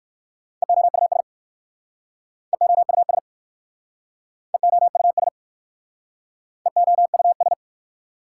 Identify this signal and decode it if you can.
Morse code, heard as a keyed tone, about 54 wpm, 690 Hz. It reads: EOUS EOUS EOUS EOUS
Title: Speed-Racing Prefixes and Suffixes 45wpm